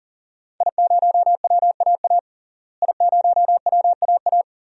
Morse code, heard as a keyed tone, about 38 wpm, 700 Hz. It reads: I0WAA I0WAA
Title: DXCC-100 Call Signs - 5 Characters - Repeated 2x - 40wpm